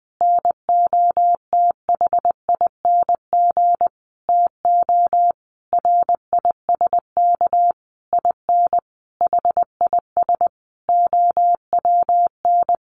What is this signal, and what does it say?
Morse code, heard as a keyed tone, about 20 wpm, 700 Hz. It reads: NOTHING TO RISK IN HIS OWN
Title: Sentences from Top 600 Words 20wpm